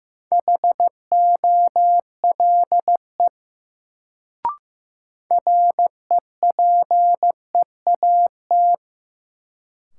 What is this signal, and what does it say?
Morse code, heard as a keyed tone, about 15 wpm, 700 Hz. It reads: HOLE REPEAT
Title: Sentences from Top 1600 Words - New Word Drill - 15wpm